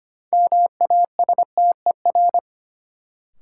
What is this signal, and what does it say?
Morse code, heard as a keyed tone, about 25 wpm, 700 Hz. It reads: MASTER
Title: Sentences from Top 1300 Words - New Word Drill - 25wpm